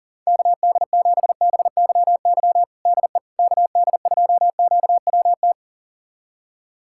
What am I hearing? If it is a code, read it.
Morse code, heard as a keyed tone, about 40 wpm, 700 Hz. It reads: KD7BYY DE KD2QWT
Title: USA Callsign DE Callsign - 6 Characters - 40wpm